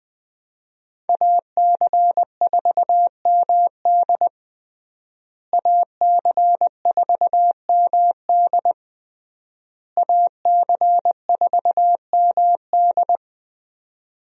Morse code, heard as a keyed tone, about 20 wpm, 700 Hz. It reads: AC4MD AC4MD AC4MD
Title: USA Call Signs - 5 Characters - Repeated 3x - 20wpm